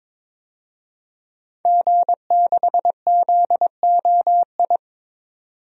Morse code, heard as a keyed tone, about 22 wpm, 700 Hz. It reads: G6ZOI
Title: DXCC-100 Call Signs - 5 Characters - 22wpm